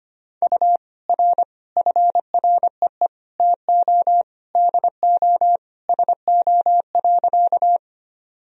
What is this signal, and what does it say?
Morse code, heard as a keyed tone, about 25 wpm, 700 Hz. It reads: U R FREE TO DO SO.
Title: Sentences from Top 500 Words and Abbreviations 25wpm